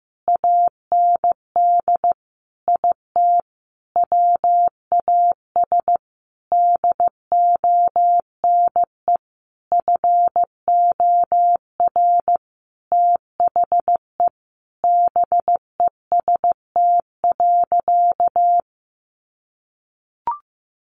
Morse code, heard as a keyed tone, about 15 wpm, 700 Hz. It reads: AND IT WAS DONE FOR THE BEST.
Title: Sentences from Top 600 Words 15wpm